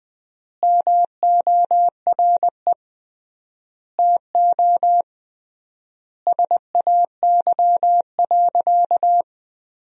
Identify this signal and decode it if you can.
Morse code, heard as a keyed tone, about 20 wpm, 700 Hz. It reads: MORE TO SAY.
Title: 3x Word-Spacing - Sentences from Top 500 Words - 20wpm